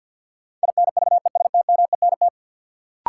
Morse code, heard as a keyed tone, about 48 wpm, 700 Hz. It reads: INVESTMENT
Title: Sentences from Top 700 Words - New Word Drill - 50wpm